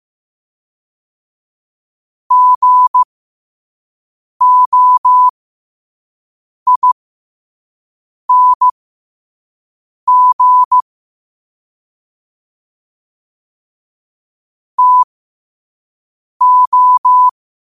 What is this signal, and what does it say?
Morse code, heard as a keyed tone, about 15 wpm, spaced as for 5 wpm, 1000 Hz. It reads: GOING TO
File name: qotdfiles / 5wpm